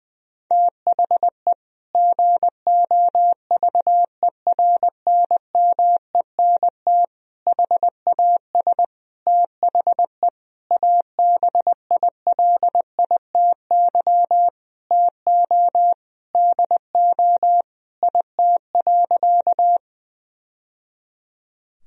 Morse code, heard as a keyed tone, about 20 wpm, 700 Hz. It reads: THE GOVERNMENT HAS THE ABILITY TO DO IT.
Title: Sentences from Top 1000 Words - Review Part 3 - 20wpm